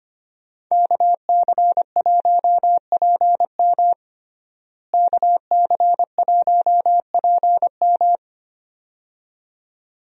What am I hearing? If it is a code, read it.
Morse code, heard as a keyed tone, about 25 wpm, 700 Hz. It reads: KC1PM KC1PM
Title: USA Call Signs - 5 Characters - Repeated 2x - 25wpm